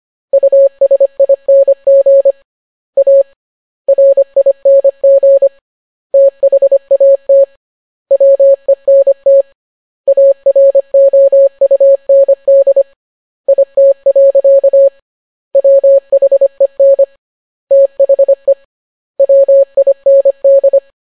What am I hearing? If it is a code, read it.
Morse code, heard as a keyed tone, about 25 wpm, 550 Hz. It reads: USING A RING THAT WENT AROUND IT. WHEN THE WIND